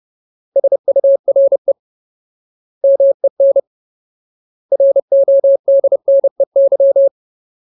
Morse code, heard as a keyed tone, about 30 wpm, 550 Hz. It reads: SURE MEN RODNEY
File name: Ultimate Word List / 30WPM Ultimate